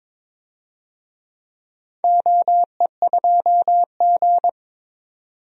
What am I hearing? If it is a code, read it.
Morse code, heard as a keyed tone, about 22 wpm, 700 Hz. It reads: OE2G
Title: DXCC-100 Call Signs - 4 Characters - 22wpm